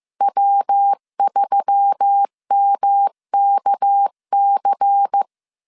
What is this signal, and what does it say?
Morse code, heard as a keyed tone, about 15 wpm, 800 Hz. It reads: W3MKC
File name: CW / DET / 1 6 1500C S C 14wpm 2951